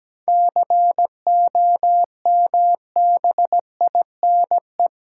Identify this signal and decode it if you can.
Morse code, heard as a keyed tone, about 17 wpm, 700 Hz. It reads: COMBINE
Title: Sentences from Top 1600 Words 17wpm